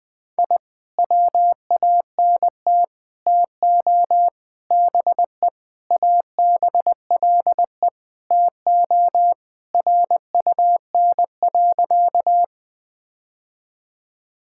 Morse code, heard as a keyed tone, about 20 wpm, 700 Hz. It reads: I WANT TO BE ABLE TO RUN.
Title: Sentences with 3-Word Phrases - 20wpm